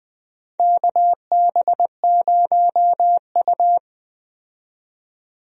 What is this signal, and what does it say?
Morse code, heard as a keyed tone, about 20 wpm, 700 Hz. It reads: KB0U
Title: USA Call Signs - 4 Characters - 20wpm